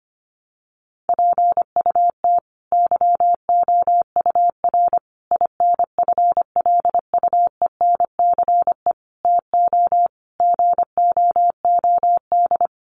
Morse code, heard as a keyed tone, about 25 wpm, 700 Hz. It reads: PUT YOUR INFLUENCE TO GOOD
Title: Sentences from Top 1100 Words 25wpm